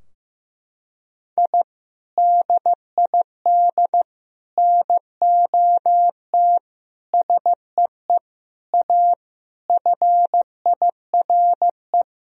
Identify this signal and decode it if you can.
Morse code, heard as a keyed tone, about 15 wpm, 700 Hz. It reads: I DID NOT SEE A FIRE
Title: Sentences from Top 1000 Words - Review Part 2 - 15wpm